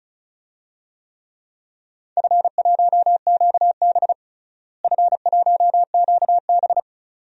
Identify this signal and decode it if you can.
Morse code, heard as a keyed tone, about 36 wpm, 700 Hz. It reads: F1QB F1QB
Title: DXCC-100 Call Signs - 4 Characters - Repeated 2x - 35wpm